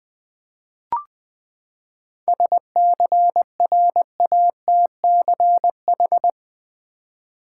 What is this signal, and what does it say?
Morse code, heard as a keyed tone, about 20 wpm, 700 Hz. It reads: SCRATCH
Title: Seven Letter Words - 20wpm